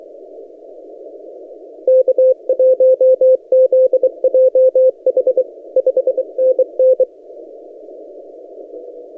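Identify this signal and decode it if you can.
Morse code, heard as a keyed tone, about 23 wpm, 520 Hz. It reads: K1ZJH 5NN
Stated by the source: XR0YG DX signal on 17 metres CW, posted by N6ML.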